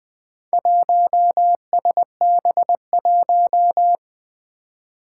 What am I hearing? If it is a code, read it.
Morse code, heard as a keyed tone, about 20 wpm, 700 Hz. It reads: 1SB1
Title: Four Challenging Letter-Number Combinations 20wpm